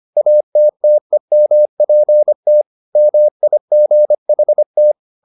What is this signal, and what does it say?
Morse code, heard as a keyed tone, about 25 wpm, 600 Hz. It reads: ATTEMPT MIGHT